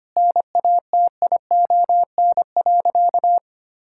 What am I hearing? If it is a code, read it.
Morse code, heard as a keyed tone, about 25 wpm, 700 Hz. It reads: NATION.
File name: practice / 25wpm / Sentences with 4-Word Phrases - 25wpm